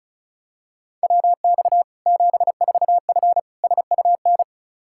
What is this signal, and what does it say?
Morse code, heard as a keyed tone, about 36 wpm, 700 Hz. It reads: WX 74F SUN
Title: Practice QSO - Exchange 2 - Info - 35wpm